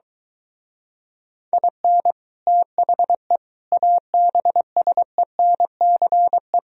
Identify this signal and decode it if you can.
Morse code, heard as a keyed tone, about 23 wpm, 700 Hz. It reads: IN THE ABSENCE